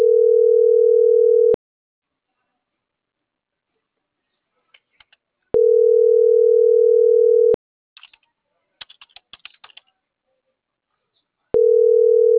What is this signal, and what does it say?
Ringback tone (North America), 440 + 480 Hz.